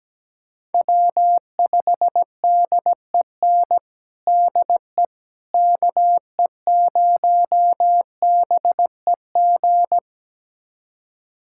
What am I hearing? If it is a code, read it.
Morse code, heard as a keyed tone, about 17 wpm, 700 Hz. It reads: W5DEN DE KE0BEG
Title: USA Callsign DE Callsign 17wpm